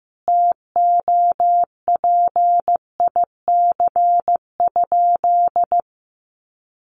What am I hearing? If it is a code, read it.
Morse code, heard as a keyed tone, about 15 wpm, 700 Hz. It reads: TOPIC?
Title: Sentences from Top 1200 Words 15wpm